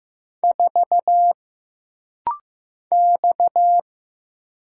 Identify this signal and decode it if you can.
Morse code, heard as a keyed tone, about 15 wpm, 700 Hz. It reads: 4 X
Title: Single Letter-Number Alt - Rapid-Fire 15wpm